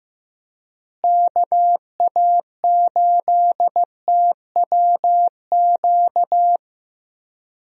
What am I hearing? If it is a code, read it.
Morse code, heard as a keyed tone, about 15 wpm, 700 Hz. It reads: KA8TWQ
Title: USA Call Signs - 6 Characters - Repeated 3x - 15wpm